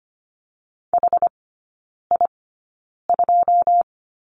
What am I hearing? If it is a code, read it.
Morse code, heard as a keyed tone, about 25 wpm, spaced as for 9 wpm, 690 Hz. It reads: HI2
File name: Lesson 14e - Number 2 - Three Characters 25wpm